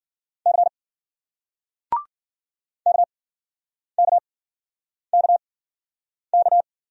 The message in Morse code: C KKKK

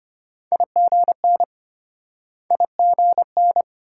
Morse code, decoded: IGN IGN

